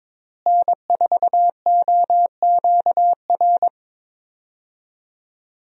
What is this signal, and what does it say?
Morse code, heard as a keyed tone, about 22 wpm, 700 Hz. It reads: N4OQR
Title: USA Call Signs - 5 Characters - Repeated 2x - 22wpm